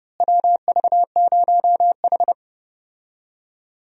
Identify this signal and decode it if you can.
Morse code, heard as a keyed tone, about 30 wpm, 700 Hz. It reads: WV0H
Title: USA Call Signs - 4 Characters - Repeated 3x - 30wpm